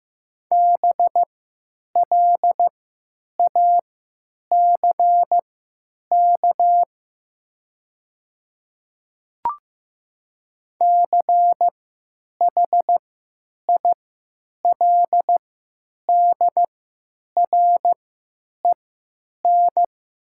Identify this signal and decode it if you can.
Morse code, heard as a keyed tone, about 15 wpm, spaced as for 8 wpm, 700 Hz. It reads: BLACK CHILDREN
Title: 3x Character Spacing - Top 500 Words - 15wpm